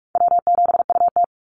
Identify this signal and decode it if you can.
Morse code, heard as a keyed tone, about 44 wpm, 700 Hz. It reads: W7UT
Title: Call Signs Repeated 3x - 45wpm